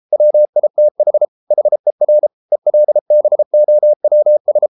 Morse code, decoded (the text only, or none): WITH HER ELBOWS